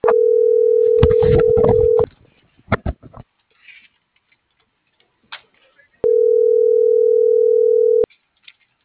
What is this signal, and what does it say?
Ringback tone (North America), 440 + 480 Hz.
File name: CENTRAL GENERAL / out misael.sky